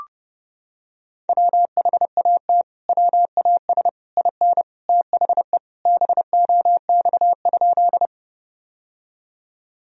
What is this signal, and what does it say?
Morse code, heard as a keyed tone, about 30 wpm, 700 Hz. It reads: WHAT WAS IN THE BOX?
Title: Sentences from Top 1000 Words - Review Part 2 - 30wpm